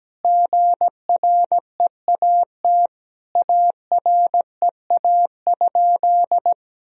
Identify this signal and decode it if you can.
Morse code, heard as a keyed tone, about 17 wpm, 700 Hz. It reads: GREAT AREA?
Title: Sentences from Top 1000 Words - Review Part 2 - 17wpm